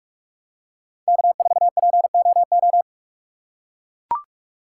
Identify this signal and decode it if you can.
Morse code, heard as a keyed tone, about 42 wpm, 700 Hz. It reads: K4POO
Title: USA Call Signs - 5 Characters - 2x Character Spacing - 45wpm